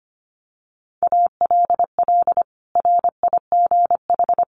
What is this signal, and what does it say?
Morse code, heard as a keyed tone, about 25 wpm, 700 Hz. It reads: ALL RIGH
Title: Sets of 2 Words 25wpm